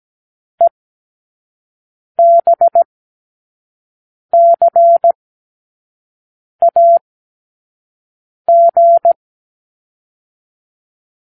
Morse code, heard as a keyed tone, about 17 wpm, spaced as for 5 wpm, 680 Hz. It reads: EBCAG
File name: k7qo / Lesson12(A-H)